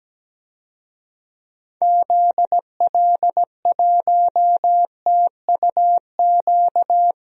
Morse code, heard as a keyed tone, about 17 wpm, 700 Hz. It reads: ZL1TUQ
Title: DXCC-100 Call Signs - 6 Characters - 17wpm